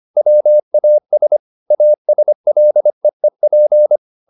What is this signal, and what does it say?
Morse code, heard as a keyed tone, about 25 wpm, 600 Hz. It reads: WAS ASLEEP
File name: the old man and the sea part 1 25wpm